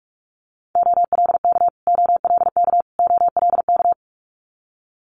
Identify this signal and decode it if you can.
Morse code, heard as a keyed tone, about 46 wpm, 700 Hz. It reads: OLK OLK OLK
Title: Multi-letter Phonemes Repeated 3x 45wpm